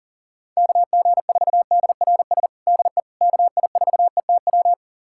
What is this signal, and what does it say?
Morse code, heard as a keyed tone, about 40 wpm, 700 Hz. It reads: KG4DRS DE KI4ETW
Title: USA Callsign DE Callsign - 6 Characters - 40wpm